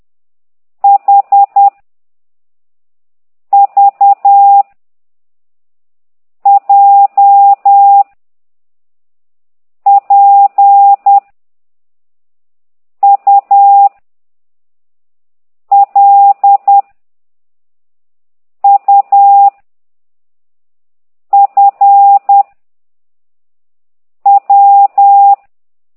Morse code, heard as a keyed tone, about 10 wpm, 800 Hz. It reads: H V J P U L U F W
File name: ct2idw / exerc9